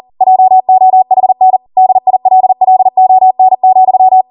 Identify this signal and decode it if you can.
Morse code, heard as a keyed tone, about 40 wpm, 750 Hz. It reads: JOHN DILLON,
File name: Morse / 40WPM